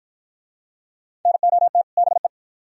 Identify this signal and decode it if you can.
Morse code, heard as a keyed tone, about 53 wpm, 700 Hz. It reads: NOT BE